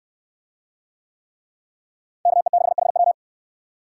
Morse code, heard as a keyed tone, about 69 wpm, 690 Hz. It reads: KE6HW